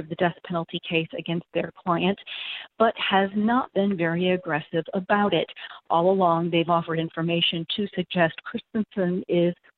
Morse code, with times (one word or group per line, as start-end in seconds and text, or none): none